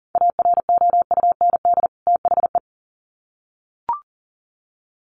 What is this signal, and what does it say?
Morse code, heard as a keyed tone, about 40 wpm, 700 Hz. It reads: AROUND THE
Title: Sets of 2 Words 40wpm